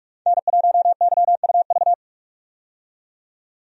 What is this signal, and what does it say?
Morse code, heard as a keyed tone, about 42 wpm, 700 Hz. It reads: N1YUV
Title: USA Call Signs - 5 Characters - 45wpm